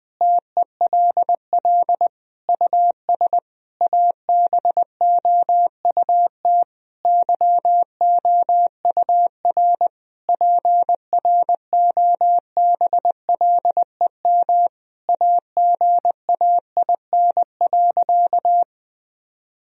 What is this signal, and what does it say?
Morse code, heard as a keyed tone, about 20 wpm, 700 Hz. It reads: TELL US ABOUT YOUR PROBLEM AGAIN.
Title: Sentences from Top 1000 Words - Review Part 3 - 20wpm